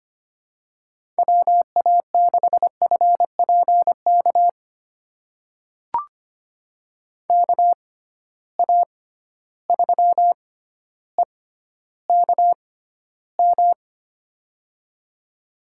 Morse code, heard as a keyed tone, about 25 wpm, spaced as for 13 wpm, 700 Hz. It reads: WA6FPK KA3EKM
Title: USA Call Signs - 6 Characters - 6x Character Spacing - 25wpm